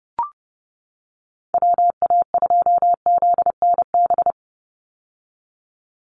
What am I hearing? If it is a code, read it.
Morse code, heard as a keyed tone, about 30 wpm, 700 Hz. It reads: WA2ZNB
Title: USA Call Signs 30wpm